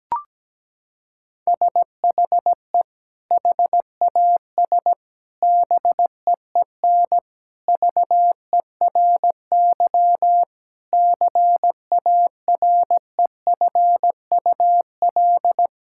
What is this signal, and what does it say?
Morse code, heard as a keyed tone, about 17 wpm, 700 Hz. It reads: SHE HAS BEEN VERY CAREFUL
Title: Sentences from Top 1600 Words 17wpm